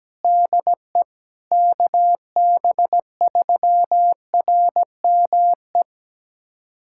Morse code, heard as a keyed tone, about 17 wpm, 700 Hz. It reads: DE KB3RME
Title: USA Callsign DE Callsign - 6 Characters - 17wpm